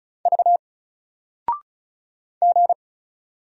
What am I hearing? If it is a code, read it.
Morse code, heard as a keyed tone, about 36 wpm, 700 Hz. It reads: V G
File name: Single-Letter-Number-35wpm